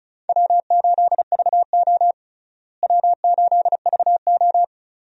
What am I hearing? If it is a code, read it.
Morse code, heard as a keyed tone, about 36 wpm, 700 Hz. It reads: W8VO W8VO
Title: USA Call Signs - 4 Characters - Repeated 2x - 35wpm